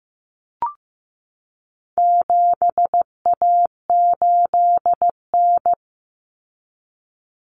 Morse code, heard as a keyed tone, about 15 wpm, 700 Hz. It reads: 7A8N
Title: Common Cut Numbers - 4-Digit 15wpm